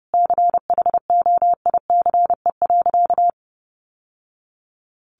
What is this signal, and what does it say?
Morse code, heard as a keyed tone, about 30 wpm, 700 Hz. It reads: CHOICE.
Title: Sentences from Top 1200 Words 30wpm